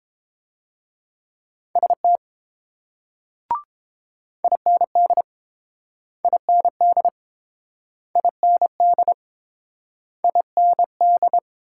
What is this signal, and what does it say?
Morse code, heard as a keyed tone, about 29 wpm, 700 Hz. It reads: ST IND IND IND IND